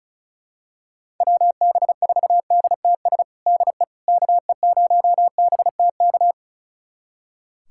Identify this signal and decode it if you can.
Morse code, heard as a keyed tone, about 34 wpm, 690 Hz. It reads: WB4DTS DE KE0BTK
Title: USA Callsign DE Callsign - 6 Characters - 35wpm